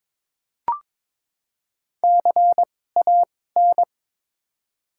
Morse code, heard as a keyed tone, about 21 wpm, spaced as for 16 wpm, 700 Hz. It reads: CAN
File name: Top 90 QSO Elements - 2x Character Spacing - 22wpm